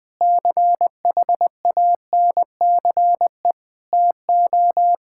Morse code, heard as a keyed tone, about 21 wpm, 700 Hz. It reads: CHANCE TO
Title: Sentences with 5-Word Phrases - 20wpm